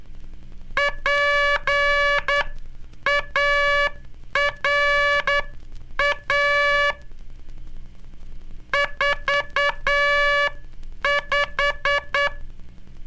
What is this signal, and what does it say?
Morse code, heard as a keyed tone, about 10 wpm, spaced as for 8 wpm, 1700 Hz, with noise. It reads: PARA 45
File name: cursos / Curso CW mp3 / CW Aula - 15